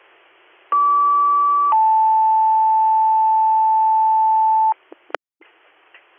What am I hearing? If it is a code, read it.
Two-tone page, heard as a one-second tone, then a three-second tone, about 1150 then 880 Hz. It is heard from 0.7 to 4.7 s.